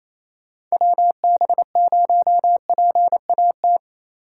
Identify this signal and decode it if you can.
Morse code, heard as a keyed tone, about 28 wpm, 700 Hz. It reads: WB0PAT
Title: USA Call Signs - 6 Characters - 8x Character Spacing - 28wpm